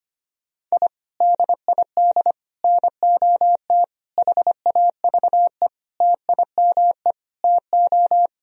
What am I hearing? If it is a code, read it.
Morse code, heard as a keyed tone, about 25 wpm, 700 Hz. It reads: I DID NOT HAVE TIME TO
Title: Sentences from Top 1000 Words - Review Part 3 - 25wpm